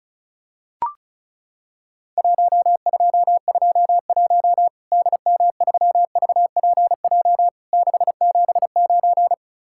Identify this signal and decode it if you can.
Morse code, heard as a keyed tone, about 34 wpm, 700 Hz. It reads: 1221 DM3VPJ 679
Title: Worked All Europe DX Contest - QTC Exchange - 35wpm